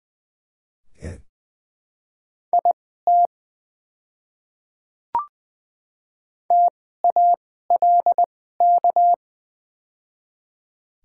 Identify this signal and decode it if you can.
Morse code, heard as a keyed tone, about 20 wpm, spaced as for 14 wpm, 690 Hz. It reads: IT TALK
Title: 2x Character Spacing - Top 500 Words - 20wpm